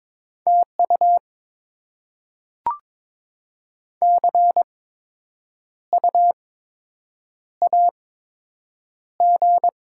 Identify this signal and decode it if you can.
Morse code, heard as a keyed tone, about 22 wpm, spaced as for 8 wpm, 700 Hz. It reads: TU CU AG
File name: Top 20 QSO Elements - 8x Character Spacing - 22wpm